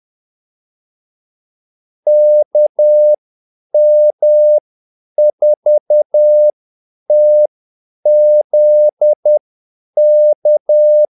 Morse code, heard as a keyed tone, about 10 wpm, spaced as for 8 wpm, 600 Hz. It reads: KM4TZK
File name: Callsigns-10-8wpm